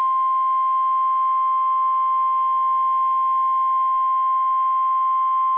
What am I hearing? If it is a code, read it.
Emergency Alert System (SAME) tones (the 1050 Hz attention tone). It is heard from 0.8 to 5.0 s.